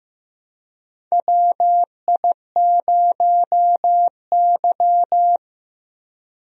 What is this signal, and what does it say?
Morse code, heard as a keyed tone, about 15 wpm, 700 Hz. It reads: WI0Y
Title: USA Call Signs - 4 Characters - Repeated 2x - 15wpm